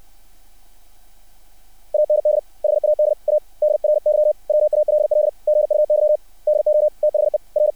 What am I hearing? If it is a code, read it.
Morse code, heard as a keyed tone, about 60 wpm, 600 Hz. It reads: AND HAD A DRY WALL GUY DO THE R
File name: archived / CRSnet